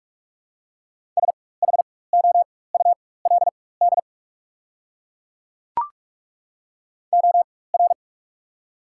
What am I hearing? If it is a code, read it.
Morse code, heard as a keyed tone, about 43 wpm, spaced as for 21 wpm, 690 Hz. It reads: SHOULD OR
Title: 4x Character Spacing - Top 300 Words - 45wpm